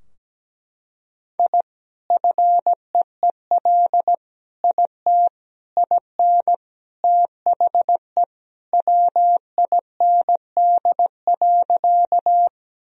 Morse code, heard as a keyed tone, about 17 wpm, 700 Hz. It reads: I FEEL IT IN THE WIND.
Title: Sentences from Top 1000 Words - Review Part 2 - 17wpm